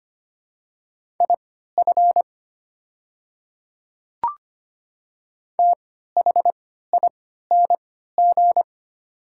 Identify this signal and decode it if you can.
Morse code, heard as a keyed tone, about 25 wpm, spaced as for 14 wpm, 700 Hz. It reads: IF THING
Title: 3x Character Spacing - Top 200 Words - 25wpm